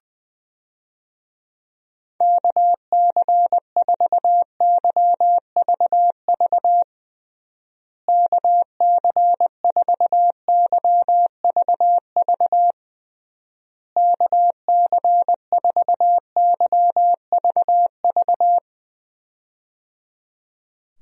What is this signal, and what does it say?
Morse code, heard as a keyed tone, about 20 wpm, 700 Hz. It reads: KC4YVV KC4YVV KC4YVV
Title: USA Call Signs - 6 Characters - Repeated 3x - 20wpm